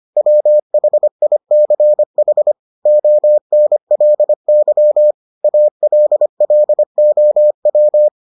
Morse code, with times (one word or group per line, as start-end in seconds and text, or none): 0.2-2.5 WHICH
2.9-5.1 ONLY
5.4-8.1 ALLOW